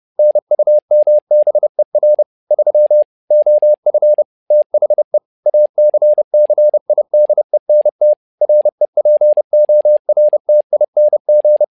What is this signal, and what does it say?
Morse code, heard as a keyed tone, about 30 wpm, 600 Hz. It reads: NUMBER 3 OF THE ACCIDENT REPORTING
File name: learning center / tower trouble / tower trouble 30wpm